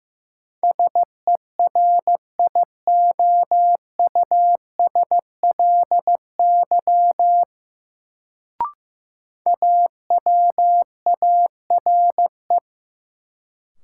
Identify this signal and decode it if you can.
Morse code, heard as a keyed tone, about 15 wpm, 700 Hz. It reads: SERIOUSLY AWARE
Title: Sentences from Top 1200 Words - New Word Drill - 15wpm